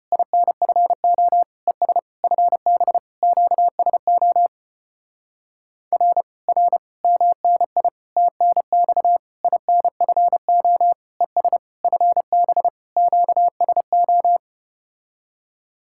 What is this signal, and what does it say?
Morse code, heard as a keyed tone, about 33 wpm, 700 Hz. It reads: INFO ES FB QSO R R MNI TNX INFO ES FB QSO